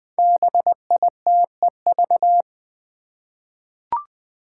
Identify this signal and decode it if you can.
Morse code, heard as a keyed tone, about 20 wpm, 700 Hz. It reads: BITEV
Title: Five Challenging Letter-Number Combinations 20wpm